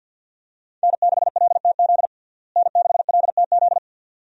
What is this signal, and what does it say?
Morse code, heard as a keyed tone, about 52 wpm, 700 Hz. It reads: N6LTZ N6LTZ
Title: USA Call Signs - 5 Characters - Repeated 3x - 50wpm